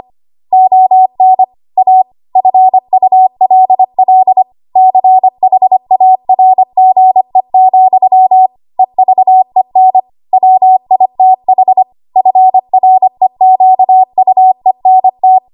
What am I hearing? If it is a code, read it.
Morse code, heard as a keyed tone, about 25 wpm, 750 Hz. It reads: ON A FULL CHARGE, EVEN WITH FREQUENT
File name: Archive / 25 WPM / 25WPM